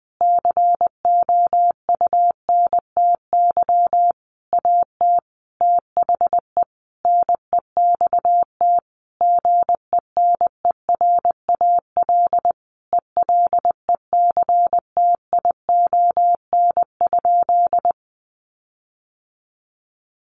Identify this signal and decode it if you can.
Morse code, heard as a keyed tone, about 20 wpm, 700 Hz. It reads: COUNTY AT THE NEXT GENERAL ELECTION?
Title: Sentences from Top 1500 Words 20wpm